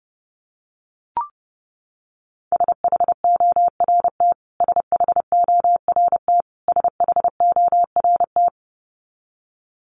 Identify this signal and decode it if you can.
Morse code, heard as a keyed tone, about 30 wpm, 700 Hz. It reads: SHORT SHORT SHORT
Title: Top 500 Words Repeated 30wpm